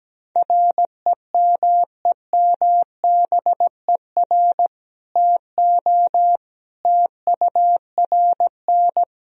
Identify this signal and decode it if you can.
Morse code, heard as a keyed tone, about 17 wpm, 700 Hz. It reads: REMEMBER TO TURN
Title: Sentences from Top 1000 Words - Review Part 3 - 17wpm